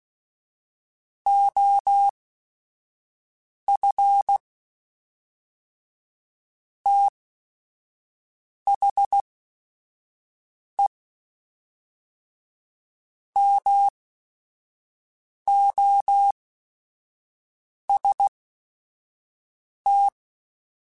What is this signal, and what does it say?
Morse code, heard as a keyed tone, about 16 wpm, spaced as for 5 wpm, 800 Hz. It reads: OF THE MOST
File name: Quotes / Quotes-mp3s / ChildAsleep @5